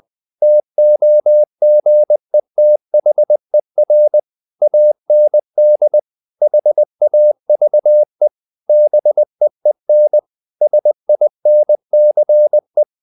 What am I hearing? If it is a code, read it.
Morse code, heard as a keyed tone, about 20 wpm, 600 Hz. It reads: TOGETHER AND HAVE BEEN SINCE
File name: the old man and the sea / the old man and the sea part 1 20wpm